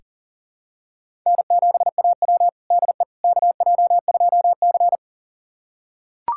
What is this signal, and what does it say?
Morse code, heard as a keyed tone, about 37 wpm, 700 Hz. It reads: N7AW DE KJ2C
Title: USA Callsign DE Callsign - 4 Characters - 40wpm